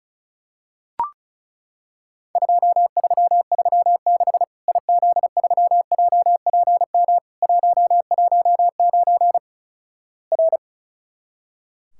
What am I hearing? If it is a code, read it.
Morse code, heard as a keyed tone, about 36 wpm, 700 Hz. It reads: 2336 IZ3JPM 119 R
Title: Worked All Europe DX Contest - QTC Exchange - 35wpm